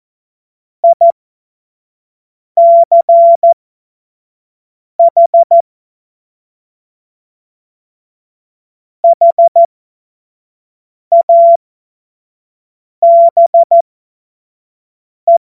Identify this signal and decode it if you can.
Morse code, heard as a keyed tone, about 14 wpm, spaced as for 5 wpm, 680 Hz. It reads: ICH HABE
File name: Morsekurs / Witze / Witz 31. August 2021, 25 F70